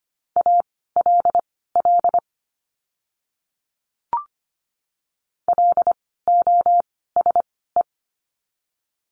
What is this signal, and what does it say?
Morse code, heard as a keyed tone, about 25 wpm, spaced as for 16 wpm, 700 Hz. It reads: ALL LOSE
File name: course / 25:10wpm / 094 Lesson 16f - Letter c - Words 25wpm